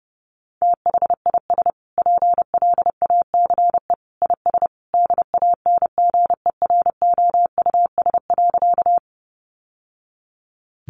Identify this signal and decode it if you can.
Morse code, heard as a keyed tone, about 30 wpm, 700 Hz. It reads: THIS PLACE IS DANGEROUS.
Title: Sentences from Top 1200 Words 30wpm